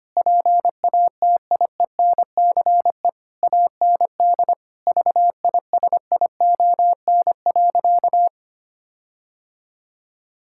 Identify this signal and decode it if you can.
Morse code, heard as a keyed tone, about 25 wpm, 700 Hz. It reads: PATIENCE AND VISION.
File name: practice / 25wpm / Difficult Sentences with Hints - 25wpm